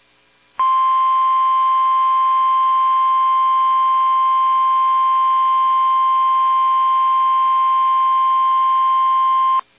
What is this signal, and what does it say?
Emergency Alert System (SAME) tones (the 1050 Hz attention tone). It is heard from 0.8 to 8.7 s.
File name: eas / KUSO / KUSO